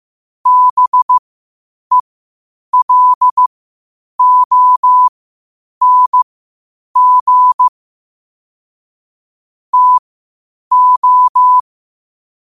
Morse code, heard as a keyed tone, about 15 wpm, spaced as for 8 wpm, 1000 Hz. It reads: BELONG TO